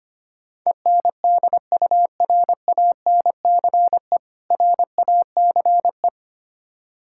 Morse code, heard as a keyed tone, about 25 wpm, 700 Hz. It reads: ENDURANCE RACE
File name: practice / 25wpm / Difficult Words - Speed-Racing with Hints - 25wpm